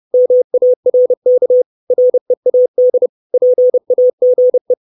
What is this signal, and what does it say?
Morse code, heard as a keyed tone, about 30 wpm, 500 Hz. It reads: MARK READ PAGE
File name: 632 Words 30WPM